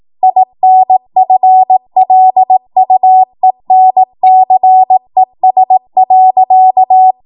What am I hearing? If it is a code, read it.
Morse code, heard as a keyed tone, about 18 wpm, 750 Hz. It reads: INFLUENCES.
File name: Morse / 18WPM